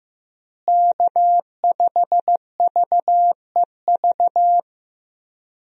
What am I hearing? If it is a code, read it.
Morse code, heard as a keyed tone, about 15 wpm, 700 Hz. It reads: K5VEV
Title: USA Call Signs - 5 Characters - Repeated 3x - 15wpm